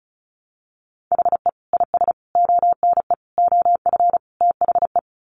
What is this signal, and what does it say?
Morse code, heard as a keyed tone, about 34 wpm, 700 Hz. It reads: HE IS ONE OF THE
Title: Sets of 5 Words Encore 35wpm